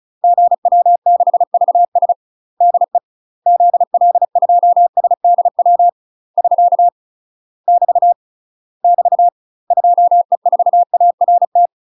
Morse code, heard as a keyed tone, about 37 wpm, 700 Hz. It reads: GW6VS DE ZL2SDW <SK> <BT> <BT> 2E4ART
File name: UK MASTER QSOs TRAINING / 35WPM / UK MASTER QSOs 0001